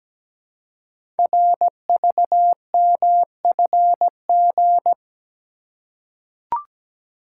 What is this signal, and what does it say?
Morse code, heard as a keyed tone, about 17 wpm, 700 Hz. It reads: RVMFG
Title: Code Groups - 5 Letter:Numbers - 17wpm